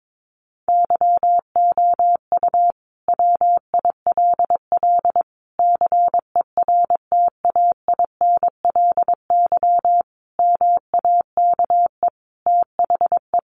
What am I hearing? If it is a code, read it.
Morse code, heard as a keyed tone, about 22 wpm, 700 Hz. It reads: YOU WILL CERTAINLY MAKE THE
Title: Sentences from Top 1400 Words 22wpm